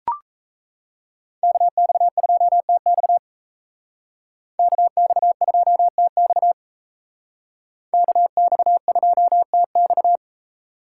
Morse code, heard as a keyed tone, about 37 wpm, 700 Hz. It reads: KX2TX KX2TX KX2TX